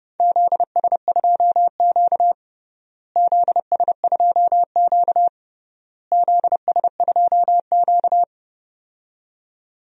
Morse code, heard as a keyed tone, about 32 wpm, 700 Hz. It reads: ZS2Q ZS2Q ZS2Q
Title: DXCC-100 Call Signs - 4 Characters - Repeated 3x - 30wpm